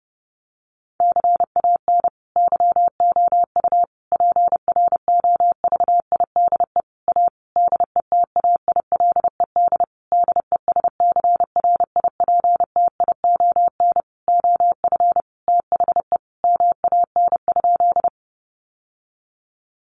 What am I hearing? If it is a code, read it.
Morse code, heard as a keyed tone, about 30 wpm, 700 Hz. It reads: CAN YOU PROVIDE A DETAILED DESCRIPTION OF THE MAN?
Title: Sentences from Top 1500 Words 30wpm